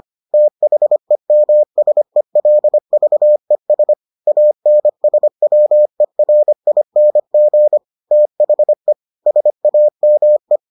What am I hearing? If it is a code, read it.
Morse code, heard as a keyed tone, about 25 wpm, 600 Hz. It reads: THEMSELVES ANSWERING THE SAME